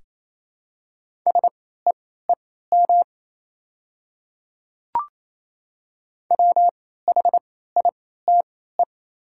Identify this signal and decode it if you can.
Morse code, heard as a keyed tone, about 28 wpm, spaced as for 16 wpm, 700 Hz. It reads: SEEM WHITE
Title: 3x Character Spacing - Top 300 Words - 28wpm